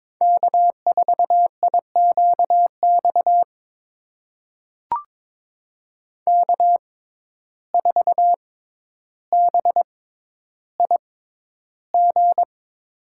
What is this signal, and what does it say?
Morse code, heard as a keyed tone, about 22 wpm, spaced as for 11 wpm, 700 Hz. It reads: K4IQX K4BIG
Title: USA Call Signs - 5 Characters - 6x Character Spacing - 22wpm